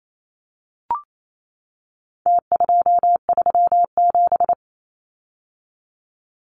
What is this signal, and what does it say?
Morse code, heard as a keyed tone, about 28 wpm, 700 Hz. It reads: T237
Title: Numbers - Contest Style - 4-Digit 28wpm